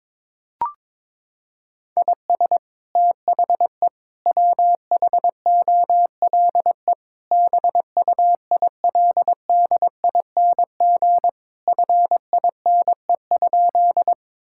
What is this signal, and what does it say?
Morse code, heard as a keyed tone, about 22 wpm, 700 Hz. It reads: IS THE WHOLE BUILDING FINE?
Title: Sentences from Top 1000 Words - Review Part 5 - 22wpm